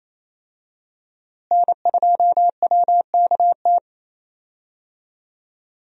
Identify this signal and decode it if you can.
Morse code, heard as a keyed tone, about 28 wpm, 700 Hz. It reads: N2WKT
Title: USA Call Signs - 5 Characters - Repeated 3x - 28wpm